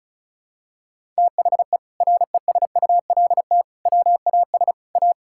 Morse code, heard as a keyed tone, about 34 wpm, 700 Hz. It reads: THE RESULT WAS A
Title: Sentences from Top 1000 Words - Review Part 6 - 35wpm